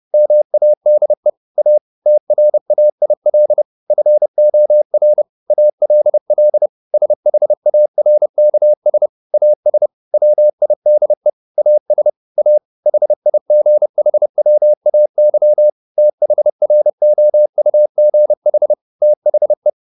Morse code, heard as a keyed tone, about 30 wpm, 600 Hz. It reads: MADE A TRAIL FOR ALL SHARKS AS WIDE AS A HIGHWAY THROUGH THE